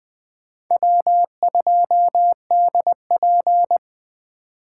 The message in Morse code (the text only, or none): W2DP